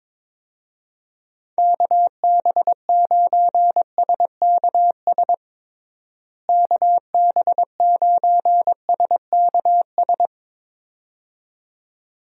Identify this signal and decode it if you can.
Morse code, heard as a keyed tone, about 22 wpm, 700 Hz. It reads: KB9SKS KB9SKS
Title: USA Call Signs - 6 Characters - Repeated 2x - 22wpm